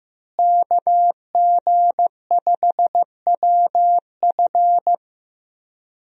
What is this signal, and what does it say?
Morse code, heard as a keyed tone, about 15 wpm, 700 Hz. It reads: KG5WF
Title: USA Call Signs - 5 Characters - Repeated 3x - 15wpm